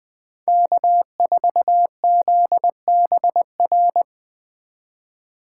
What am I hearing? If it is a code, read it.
Morse code, heard as a keyed tone, about 20 wpm, 700 Hz. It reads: K4ZBR
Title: Speed Racing - USA 1x3 Call Signs 17wpm